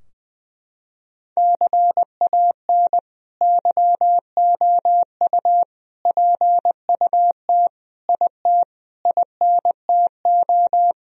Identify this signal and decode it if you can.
Morse code, heard as a keyed tone, about 20 wpm, 700 Hz. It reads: CAN YOU PUT IT INTO
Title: Sentences from Top 1000 Words - Review Part 4 - 20wpm